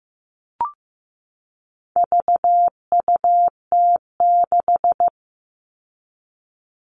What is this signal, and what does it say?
Morse code, heard as a keyed tone, about 15 wpm, 700 Hz. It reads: VUT6
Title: All Cut Numbers - 4-Digit 15wpm